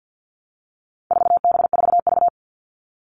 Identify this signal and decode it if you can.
Morse code, heard as a keyed tone, about 50 wpm, 700 Hz. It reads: 4B4V
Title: All Cut Numbers - 4-Digit 50wpm